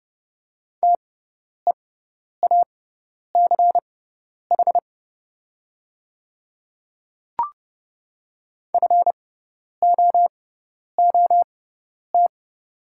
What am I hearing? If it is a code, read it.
Morse code, heard as a keyed tone, about 31 wpm, spaced as for 10 wpm, 700 Hz. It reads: TEACH FOOT
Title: 6x Character Spacing - Top 500 Words - 30wpm